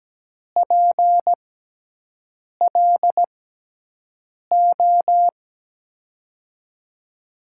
Morse code, heard as a keyed tone, about 17 wpm, spaced as for 6 wpm, 700 Hz. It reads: PLO
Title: USA Call Signs - 6 Characters - 6x Character Spacing - 17wpm